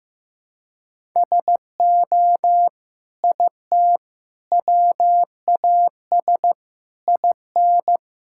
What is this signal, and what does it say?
Morse code, heard as a keyed tone, about 15 wpm, 700 Hz. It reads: SO IT WAS IN
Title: Sentences from Top 1000 Words - Review Part 5 - 15wpm